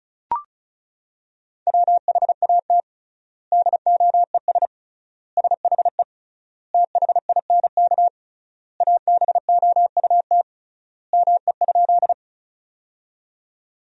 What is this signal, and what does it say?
Morse code, heard as a keyed tone, about 36 wpm, 700 Hz. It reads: WHAT DOES SHE THINK ABOUT ME?
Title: 3x Word-Spacing - Sentences from Top 200 Words - 35wpm